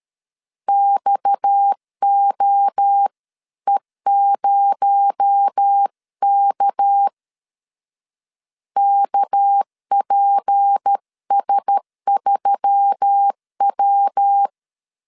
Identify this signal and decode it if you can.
Morse code, heard as a keyed tone, about 13 wpm, 800 Hz. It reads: XOE0K KPS3W
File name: CW / DET / 1 6 1500C S B 12wpm 3431